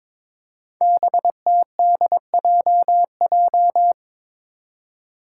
Morse code, heard as a keyed tone, about 22 wpm, 700 Hz. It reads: BTDJJ
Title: Five Challenging Letter-Number Combinations 22wpm